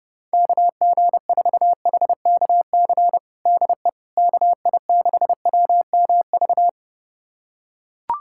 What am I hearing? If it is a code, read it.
Morse code, heard as a keyed tone, about 30 wpm, 700 Hz. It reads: KG4HKC DE KI6WMV
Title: USA Callsign DE Callsign 30wpm